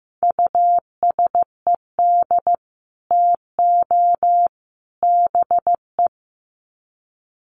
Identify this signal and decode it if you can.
Morse code, heard as a keyed tone, about 15 wpm, 700 Hz. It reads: USED TO BE
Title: Sets of 3 Words 15wpm